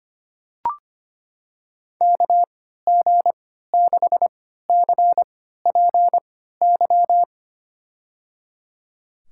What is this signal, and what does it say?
Morse code, heard as a keyed tone, about 25 wpm, spaced as for 14 wpm, 700 Hz. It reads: KG6CPY
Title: USA Call Signs - 6 Characters - 3x Character Spacing - 25wpm